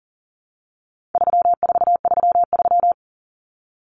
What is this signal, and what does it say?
Morse code, heard as a keyed tone, about 39 wpm, 700 Hz. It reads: 3433
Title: Numbers - 4-Digit 40wpm